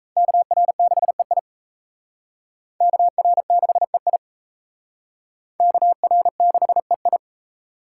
Morse code, heard as a keyed tone, about 35 wpm, 700 Hz. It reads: KR6EI KR6EI KR6EI